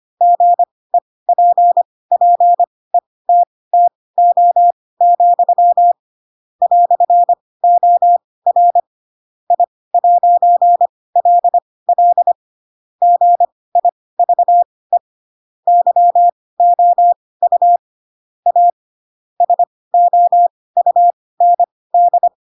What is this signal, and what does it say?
Morse code, heard as a keyed tone, about 25 wpm, spaced as for 18 wpm, 700 Hz. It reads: GEPPETTO, "OR I'LL GIVE YOU A SOUND
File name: Pinocchio / Pinocchio ch7 12 25 ik1hge